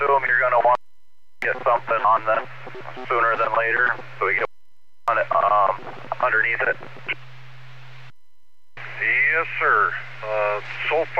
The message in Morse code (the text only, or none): none